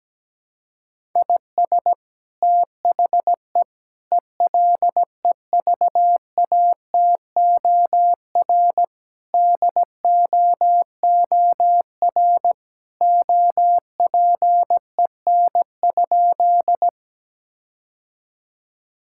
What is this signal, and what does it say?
Morse code, heard as a keyed tone, about 17 wpm, 700 Hz. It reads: IS THE ELEVATOR DOOR OPEN?
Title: Sentences from Top 1600 Words 17wpm